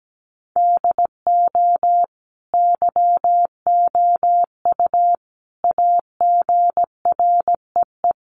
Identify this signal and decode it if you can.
Morse code, heard as a keyed tone, about 17 wpm, 700 Hz. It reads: DO YOU AGREE
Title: Sets of 5 Words Encore 17wpm